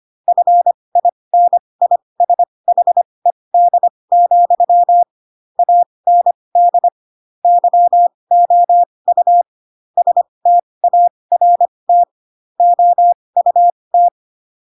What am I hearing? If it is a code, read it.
Morse code, heard as a keyed tone, about 25 wpm, spaced as for 20 wpm, 700 Hz. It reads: FINISHED, AND YOU START OUT
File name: Pinocchio / Pinocchio ch3 15 25 ik1hge